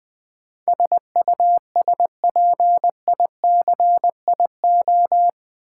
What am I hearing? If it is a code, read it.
Morse code, heard as a keyed tone, about 20 wpm, 700 Hz. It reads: SUSPICIO
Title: Build A Word - 4 Letters At A Time - 10-Letter Words - 20wpm